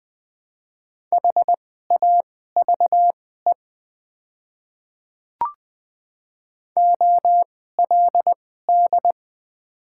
Morse code, heard as a keyed tone, about 20 wpm, spaced as for 14 wpm, 690 Hz. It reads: HAVE OLD
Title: 2x Character Spacing - Top 300 Words - 20wpm